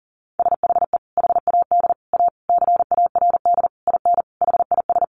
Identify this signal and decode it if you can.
Morse code, heard as a keyed tone, about 40 wpm, 700 Hz. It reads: SHE HAD A CARD IN HIS
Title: Sentences from Top 600 Words 40wpm